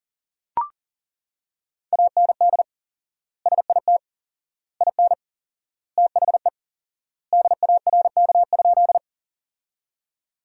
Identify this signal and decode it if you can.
Morse code, heard as a keyed tone, about 40 wpm, 700 Hz. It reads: AND SIT IN THE DARK?
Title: 4x Word-Spacing - Sentences from Top 500 Words - 40wpm